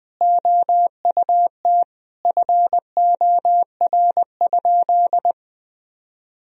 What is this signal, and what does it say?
Morse code, heard as a keyed tone, about 20 wpm, 700 Hz. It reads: OUT FOR?
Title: Sentences from Top 1000 Words - Review Part 3 - 20wpm